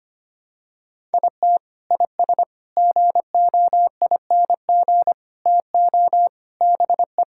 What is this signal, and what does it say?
Morse code, heard as a keyed tone, about 25 wpm, 700 Hz. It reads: IT IS GOING TO BE
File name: practice / 25wpm / Sentences with 4-Word Phrases - 25wpm